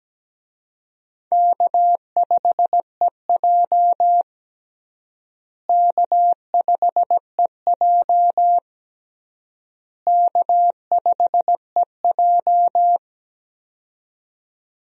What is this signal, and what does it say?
Morse code, heard as a keyed tone, about 17 wpm, 700 Hz. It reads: K5EJ K5EJ K5EJ
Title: USA Call Signs - 4 Characters - Repeated 3x - 17wpm